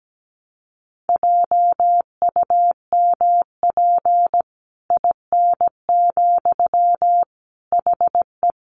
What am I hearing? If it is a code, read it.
Morse code, heard as a keyed tone, about 17 wpm, 700 Hz. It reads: JUMP IN, HE
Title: Sentences from Top 1300 Words 17wpm